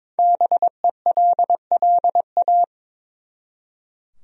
Morse code, heard as a keyed tone, about 22 wpm, 700 Hz. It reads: BELLA
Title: USA Common Names - 22wpm